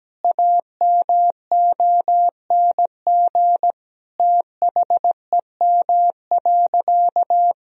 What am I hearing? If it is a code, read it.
Morse code, heard as a keyed tone, about 17 wpm, 700 Hz. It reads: AMONG THEM.
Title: Sentences from Top 1000 Words - Review Part 2 - 17wpm